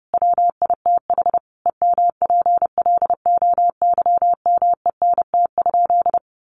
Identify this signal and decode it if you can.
Morse code, heard as a keyed tone, about 30 wpm, 700 Hz. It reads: WITH EMPLOYMENT?
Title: Sentences from Top 1500 Words 30wpm